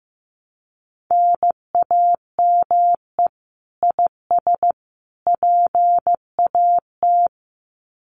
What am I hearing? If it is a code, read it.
Morse code, heard as a keyed tone, about 15 wpm, 700 Hz. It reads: NAME IS PAT
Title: Common QSO Phrases 15wpm